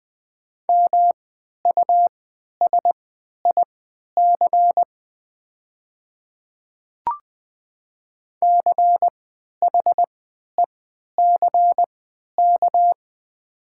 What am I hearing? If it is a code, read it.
Morse code, heard as a keyed tone, about 20 wpm, spaced as for 11 wpm, 700 Hz. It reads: MUSIC CHECK
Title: 3x Character Spacing - Top 500 Words - 20wpm